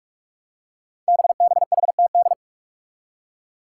Morse code, heard as a keyed tone, about 44 wpm, 700 Hz. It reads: BBHTD
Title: Five Challenging Letter-Number Combinations 45wpm